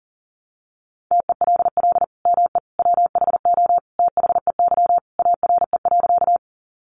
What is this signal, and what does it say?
Morse code, heard as a keyed tone, about 37 wpm, 700 Hz. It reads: TELL ME WHO THEY ARE.
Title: Sentences from Top 700 Words 40wpm